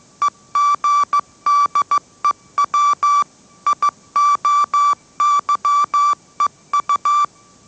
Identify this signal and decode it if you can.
Morse code, heard as a keyed tone, about 15 wpm, 1160 Hz. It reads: EGDEW IOYEU
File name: mrx / mrx065 16wpm